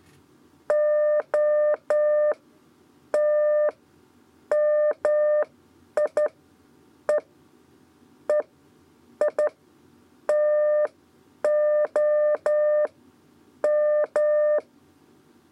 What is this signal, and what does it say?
Morse code, heard as a keyed tone, about 11 wpm, spaced as for 9 wpm, 600 Hz. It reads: O T MI E E I TO M
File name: Lezione trasmessa 2 lez 2 exe